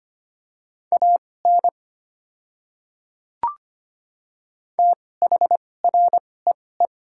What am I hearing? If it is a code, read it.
Morse code, heard as a keyed tone, about 25 wpm, spaced as for 18 wpm, 690 Hz. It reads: AN THREE
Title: 2x Character Spacing - Top 500 Words - 25wpm